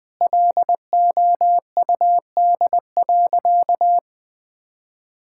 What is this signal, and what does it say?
Morse code, heard as a keyed tone, about 20 wpm, 700 Hz. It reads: LOUD.
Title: Sentences with 2-Word Phrases - 20wpm